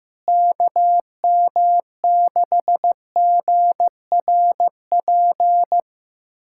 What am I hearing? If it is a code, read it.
Morse code, heard as a keyed tone, about 15 wpm, 700 Hz. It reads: KM6GRP
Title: USA Callsign DE Callsign - 6 Characters - 15wpm